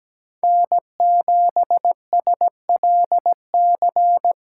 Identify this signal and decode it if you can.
Morse code, heard as a keyed tone, about 17 wpm, 700 Hz. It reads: N7SLC
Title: SOTA QSO Practice 17wpm